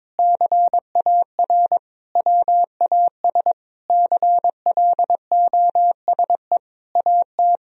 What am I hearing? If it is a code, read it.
Morse code, heard as a keyed tone, about 21 wpm, 700 Hz. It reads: CAR WAS CLOSE AT
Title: Sentences from Top 1000 Words - Review Part 1 - 22wpm